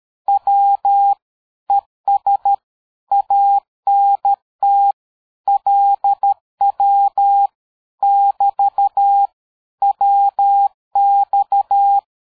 Morse code, heard as a keyed tone, about 13 wpm, 800 Hz. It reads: W ES ANT LW <BT> WX
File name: dk5ke / QSO 60BpM